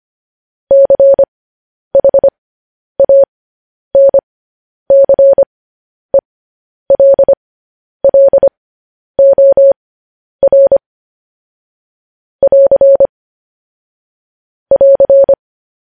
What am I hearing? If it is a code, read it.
Morse code, heard as a keyed tone, about 25 wpm, spaced as for 10 wpm, 550 Hz. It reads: CHANCELLOR <AR> <AR>
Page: news-headlines-in-morse-code-at-10-wpm